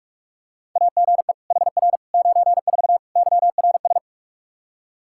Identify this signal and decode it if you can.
Morse code, heard as a keyed tone, about 46 wpm, 700 Hz. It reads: AGE HR 94 YRS
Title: Practice QSO - Exchange 3 - More Info - 45wpm